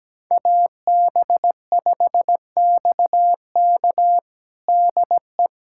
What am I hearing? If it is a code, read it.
Morse code, heard as a keyed tone, about 17 wpm, 700 Hz. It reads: AB5XK DE
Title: USA Callsign DE Callsign - 5 Characters - 17wpm